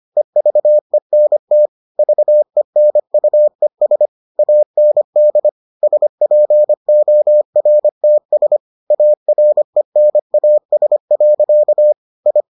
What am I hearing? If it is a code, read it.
Morse code, heard as a keyed tone, about 25 wpm, 600 Hz. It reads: EVENT VENUES AND SPORTS ARENAS. I